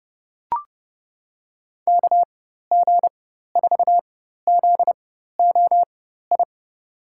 Morse code, heard as a keyed tone, about 30 wpm, spaced as for 14 wpm, 700 Hz. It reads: KG4ZOI
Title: USA Call Signs - 6 Characters - 4x Character Spacing - 30wpm